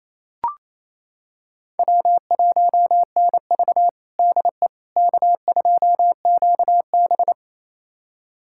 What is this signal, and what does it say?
Morse code, heard as a keyed tone, about 28 wpm, 700 Hz. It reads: W1NV DE K2QB
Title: USA Callsign DE Callsign - 4 Characters - 28wpm